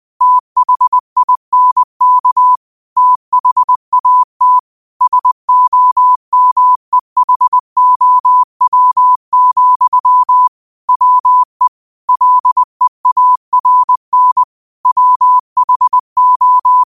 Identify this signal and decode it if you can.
Morse code, heard as a keyed tone, about 20 wpm, 1000 Hz. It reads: THINK THAT SOMEHOW, WE LEARN WHO
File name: qotdfiles / 20wpm